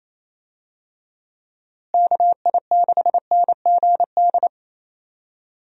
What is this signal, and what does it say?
Morse code, heard as a keyed tone, about 28 wpm, 700 Hz. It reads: KI6NGD
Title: USA Call Signs - 6 Characters - Repeated 3x - 28wpm